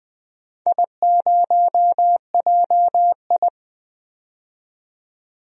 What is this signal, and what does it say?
Morse code, heard as a keyed tone, about 20 wpm, 700 Hz. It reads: I0JI
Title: DXCC-100 Call Signs - 4 Characters - 20wpm